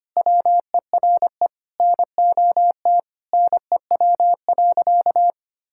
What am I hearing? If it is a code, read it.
Morse code, heard as a keyed tone, about 25 wpm, 700 Hz. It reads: WERE NOT NEW.
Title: Sentences from Top 1600 Words 25wpm